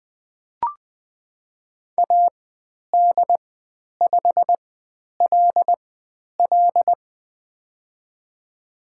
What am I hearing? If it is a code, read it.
Morse code, heard as a keyed tone, about 20 wpm, spaced as for 10 wpm, 700 Hz. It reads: AD5LL
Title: 102 Lesson 17g - Letter u - Call Signs 20wpm